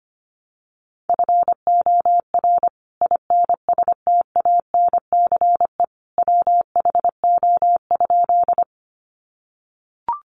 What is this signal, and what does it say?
Morse code, heard as a keyed tone, about 25 wpm, 700 Hz. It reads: FOR INSTANCE WHO?
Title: Sentences from Top 800 Words 25wpm